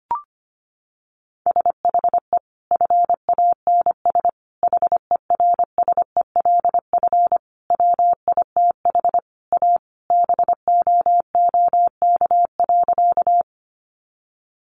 Morse code, heard as a keyed tone, about 25 wpm, 700 Hz. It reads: SHE FANS HERSELF WITH A BOOK.
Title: Additional Easy Phrases 25wpm